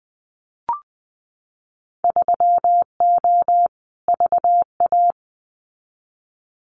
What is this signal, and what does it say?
Morse code, heard as a keyed tone, about 20 wpm, 700 Hz. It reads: 3O VA
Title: Winter Field Day Exchanges 20wpm